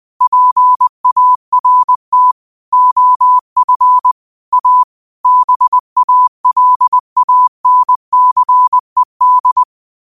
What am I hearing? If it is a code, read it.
Morse code, heard as a keyed tone, about 20 wpm, 1000 Hz. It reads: PART OF A BALANCED
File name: qotdfiles / 20wpm